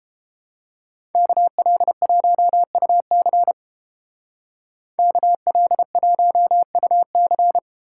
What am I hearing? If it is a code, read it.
Morse code, heard as a keyed tone, about 30 wpm, 700 Hz. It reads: KL1UC KL1UC